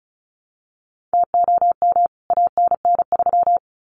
Morse code, heard as a keyed tone, about 34 wpm, 700 Hz. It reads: TOM ANN3
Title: CWT Exchange 35wpm